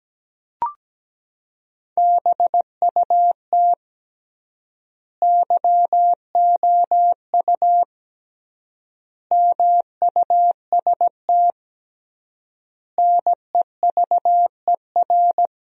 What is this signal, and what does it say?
Morse code, heard as a keyed tone, about 17 wpm, 700 Hz. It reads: BUT YOU MUST NEVER
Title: 3x Word-Spacing - Sentences from Top 300 Words - 17wpm